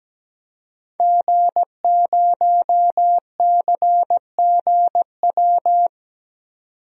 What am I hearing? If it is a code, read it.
Morse code, heard as a keyed tone, about 17 wpm, 700 Hz. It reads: G0CGW
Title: DXCC-100 Call Signs - 5 Characters - 17wpm